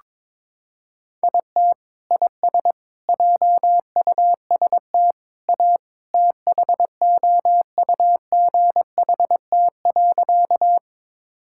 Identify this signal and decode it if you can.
Morse code, heard as a keyed tone, about 22 wpm, 700 Hz. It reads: IT IS JUST A THOUGHT.
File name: practice / 22wpm / Sentences with 2-Word Phrases - 22wpm